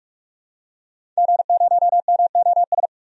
Morse code, heard as a keyed tone, about 44 wpm, 690 Hz. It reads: G0MOS
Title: DXCC-100 Call Signs - 5 Characters - 45wpm